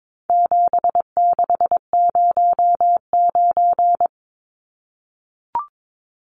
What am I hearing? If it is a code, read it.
Morse code, heard as a keyed tone, about 22 wpm, 700 Hz. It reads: 7609
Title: Numbers - 4-Digit 22wpm